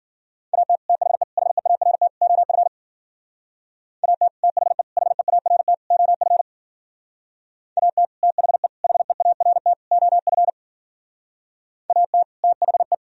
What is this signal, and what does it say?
Morse code, heard as a keyed tone, about 49 wpm, 700 Hz. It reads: AT THE HEART OF AT THE HEART OF AT THE HEART OF AT THE
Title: Speed-Racing - Sets of 4 Words 40wpm